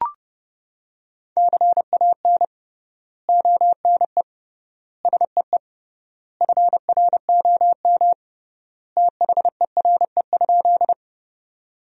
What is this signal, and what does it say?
Morse code, heard as a keyed tone, about 31 wpm, 700 Hz. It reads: CAN ONE SEE FROM THERE?
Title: 3x Word-Spacing - Sentences from Top 100 Words - 30wpm